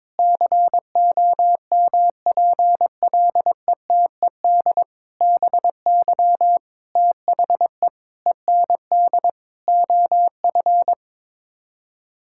Morse code, heard as a keyed tone, about 21 wpm, 700 Hz. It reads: COMPLETED BY THE END OF
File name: practice / 22wpm / Sets of 8 Words - 22wpm